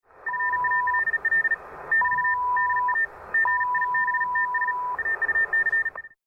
Morse code, heard as a keyed tone, about 37 wpm, 1800 Hz, with radio noise. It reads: FALL <BT> DA GIBTS AUC
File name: FunktechnikAusstellung-Amateurfunkbetriebsart-schnelles-Morsezeichen